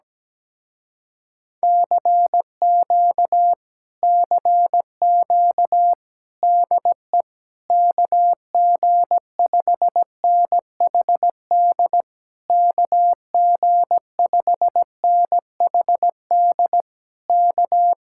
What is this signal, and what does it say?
Morse code, heard as a keyed tone, about 17 wpm, 700 Hz. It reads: CQ CQ DE KG5NHD KG5NHD K
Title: Speed-Racing Common QSO Phrases - 17wpm